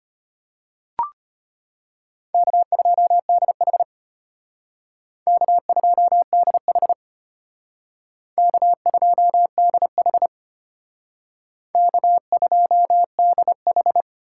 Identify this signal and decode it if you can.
Morse code, heard as a keyed tone, about 33 wpm, 700 Hz. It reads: K2DH K2DH K2DH K2DH